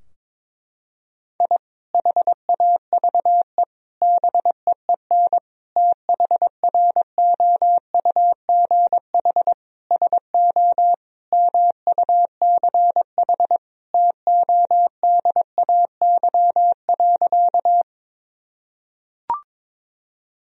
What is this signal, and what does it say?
Morse code, heard as a keyed tone, about 22 wpm, 700 Hz. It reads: I HAVE BEEN THROUGH SO MUCH TODAY.
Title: Sentences from Top 1000 Words - Review Part 4 - 22wpm